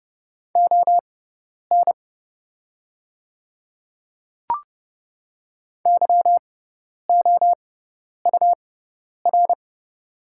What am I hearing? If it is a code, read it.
Morse code, heard as a keyed tone, about 30 wpm, spaced as for 10 wpm, 700 Hz. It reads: ON YOUR
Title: 6x Character Spacing - Top 100 Words - 30wpm